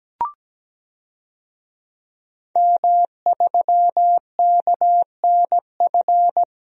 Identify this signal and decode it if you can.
Morse code, heard as a keyed tone, about 17 wpm, 700 Hz. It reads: M3KNF
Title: DXCC-100 Call Signs - 5 Characters - Repeated 2x - 17wpm